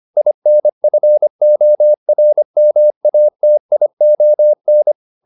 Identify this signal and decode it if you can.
Morse code, heard as a keyed tone, about 25 wpm, 600 Hz. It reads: INFORMATION